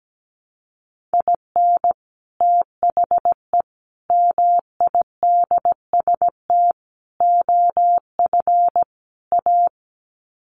Morse code, heard as a keyed tone, about 17 wpm, 700 Hz. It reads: IN THE MIDST OF A
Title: Sets of 5 Words 17wpm